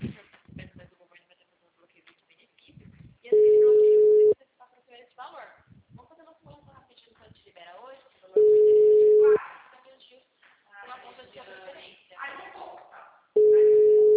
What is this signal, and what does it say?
Ringback tone (Europe, 425 Hz), with speech.
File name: out- -2002- - - .51042